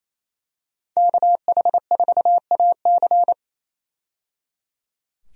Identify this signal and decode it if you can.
Morse code, heard as a keyed tone, about 28 wpm, 700 Hz. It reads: KH4AC
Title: USA Call Signs - 5 Characters - 28wpm